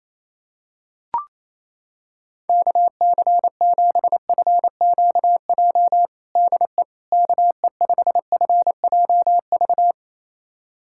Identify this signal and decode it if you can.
Morse code, heard as a keyed tone, about 27 wpm, 700 Hz. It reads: KC7FQJ DE KE5FJV
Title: USA Callsign DE Callsign - 6 Characters - 28wpm